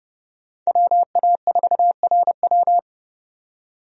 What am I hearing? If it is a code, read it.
Morse code, heard as a keyed tone, about 30 wpm, 700 Hz. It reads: WA4RW
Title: USA Call Signs - 5 Characters - 30wpm